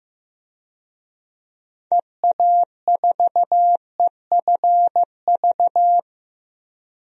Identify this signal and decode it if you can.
Morse code, heard as a keyed tone, about 14 wpm, 700 Hz. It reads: EA4EFV
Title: DXCC-100 Call Signs - 6 Characters - 15wpm